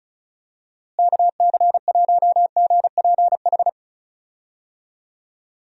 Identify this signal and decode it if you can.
Morse code, heard as a keyed tone, about 34 wpm, 700 Hz. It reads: KC1GPH
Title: USA Call Signs - 6 Characters - Repeated 3x - 35wpm